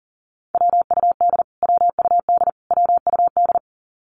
Morse code, heard as a keyed tone, about 40 wpm, 700 Hz. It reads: WUD WUD WUD
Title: Abbreviations Repeated 3x - 40wpm